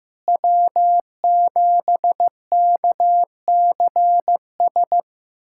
Morse code, heard as a keyed tone, about 15 wpm, 700 Hz. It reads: W7KCS
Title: USA Callsign DE Callsign - 5 Characters - 15wpm